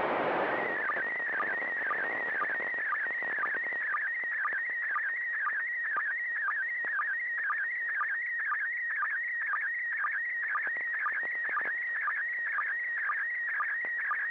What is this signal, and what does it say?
SSTV (slow-scan television), PD 120.
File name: SSTV ISS